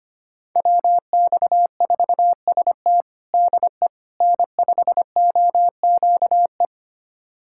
Morse code, heard as a keyed tone, about 25 wpm, 700 Hz. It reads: WX4ST DE N5OQE
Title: USA Callsign DE Callsign 25wpm